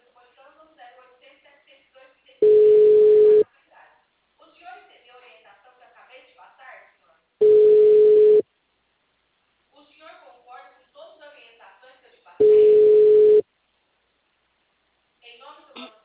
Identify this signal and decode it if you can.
Ringback tone (Europe, 425 Hz), with speech.
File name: out- -2039- - - .25520